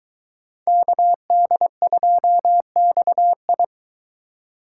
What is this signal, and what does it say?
Morse code, heard as a keyed tone, about 22 wpm, 700 Hz. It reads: KD2XI